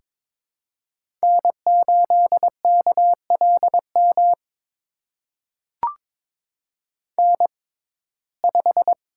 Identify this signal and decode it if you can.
Morse code, heard as a keyed tone, about 22 wpm, spaced as for 16 wpm, 700 Hz. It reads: N8KLM N5
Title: USA Call Signs - 6x Character Spacing - 22wpm